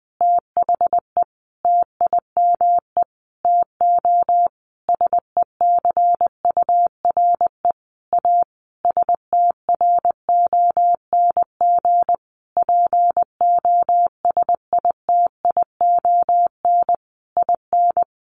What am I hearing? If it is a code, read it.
Morse code, heard as a keyed tone, about 20 wpm, 700 Hz. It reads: THE TIME TO SECURE A STRONG POSITION IN
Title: Sentences from Top 1200 Words 20wpm